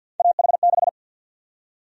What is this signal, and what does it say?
Morse code, heard as a keyed tone, about 52 wpm, 700 Hz. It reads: AH6
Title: USA Call Sign Prefixes - 50wpm